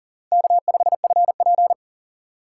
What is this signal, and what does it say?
Morse code, heard as a keyed tone, about 38 wpm, 690 Hz. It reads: K5FP
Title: USA Call Signs - 4 Characters - 2x Character Spacing - 40wpm